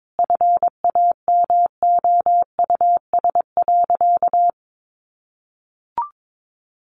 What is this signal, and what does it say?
Morse code, heard as a keyed tone, about 22 wpm, 700 Hz. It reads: FAMOUS.
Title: Sentences from Top 1500 Words 22wpm